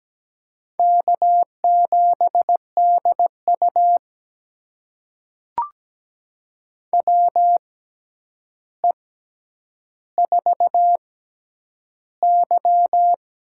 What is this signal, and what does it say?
Morse code, heard as a keyed tone, about 17 wpm, spaced as for 9 wpm, 700 Hz. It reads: K7DU WE4Y
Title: USA Call Signs - 6x Character Spacing - 17wpm